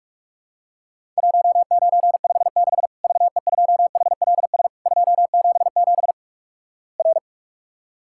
Morse code, heard as a keyed tone, about 46 wpm, 700 Hz. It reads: 1956 VE2HLS 277 R
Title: Worked All Europe DX Contest - QTC Exchange - 45wpm